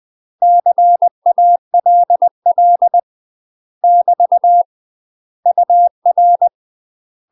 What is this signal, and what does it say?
Morse code, heard as a keyed tone, about 20 wpm, 700 Hz. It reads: CALL <BT> UR
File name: UK MASTER QSOs TRAINING / 20WPM / UK MASTER QSOs 0068